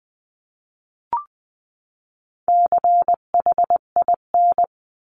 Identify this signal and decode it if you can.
Morse code, heard as a keyed tone, about 20 wpm, 700 Hz. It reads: CHIN
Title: 259 Lesson 42d - Words 18-20wpm